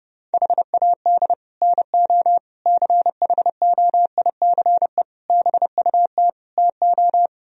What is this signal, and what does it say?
Morse code, heard as a keyed tone, about 30 wpm, 700 Hz. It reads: HAD NO CHOICE BUT TO